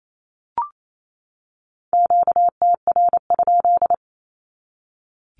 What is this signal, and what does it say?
Morse code, heard as a keyed tone, about 28 wpm, 700 Hz. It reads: QTR?
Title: Q-Codes 28wpm